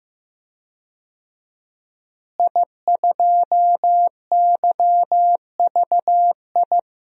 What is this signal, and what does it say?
Morse code, heard as a keyed tone, about 16 wpm, 700 Hz. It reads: I2YVI
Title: DXCC-100 Call Signs - 5 Characters - Repeated 3x - 15wpm